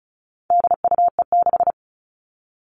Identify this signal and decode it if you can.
Morse code, heard as a keyed tone, about 34 wpm, 700 Hz. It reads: DUE6
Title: All Cut Numbers - 4-Digit 35wpm